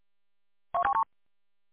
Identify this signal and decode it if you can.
There is no code.